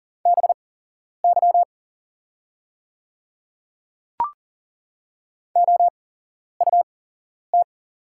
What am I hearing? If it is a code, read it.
Morse code, heard as a keyed tone, about 40 wpm, spaced as for 11 wpm, 690 Hz. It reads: BY OUT
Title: 8x Character Spacing - Top 300 Words - 40wpm